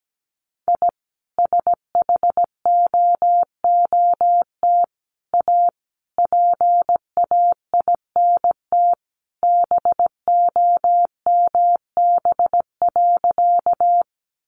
Morse code, heard as a keyed tone, about 17 wpm, 700 Hz. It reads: I SHOOT A PAINT BOMB.
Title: A Sound Of Thunder 17wpm